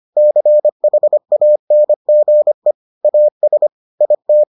Morse code, heard as a keyed tone, about 25 wpm, 600 Hz. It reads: CHANGE AS IT